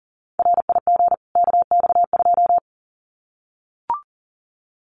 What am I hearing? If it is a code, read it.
Morse code, heard as a keyed tone, about 40 wpm, 700 Hz. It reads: RIG KX2
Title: Common QSO Phrases 40wpm